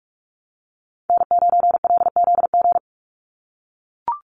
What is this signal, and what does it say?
Morse code, heard as a keyed tone, about 44 wpm, 700 Hz. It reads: N9LZG
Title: USA Call Signs 45wpm